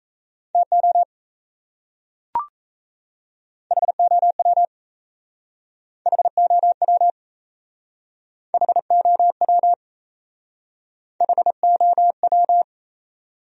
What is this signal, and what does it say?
Morse code, heard as a keyed tone, about 35 wpm, 700 Hz. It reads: TO HOW HOW HOW HOW
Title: Speed-Racing Top 200 Words 28wpm